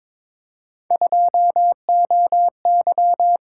Morse code, heard as a keyed tone, about 22 wpm, 700 Hz. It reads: 2OY
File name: practice / 22wpm / Three Letter-Number Combinations - 22wpm